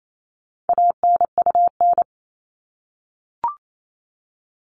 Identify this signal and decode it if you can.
Morse code, heard as a keyed tone, about 28 wpm, 700 Hz. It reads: ANUN
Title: All Cut Numbers - 4-Digit 28wpm